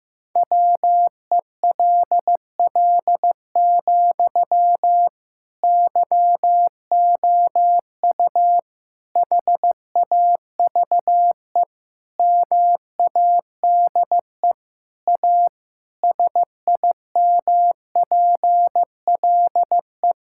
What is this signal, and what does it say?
Morse code, heard as a keyed tone, about 15 wpm, 700 Hz. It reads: WELL, YOU HAVE MADE A SIMPLE
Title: Sentences from Top 1000 Words - Review Part 6 - 15wpm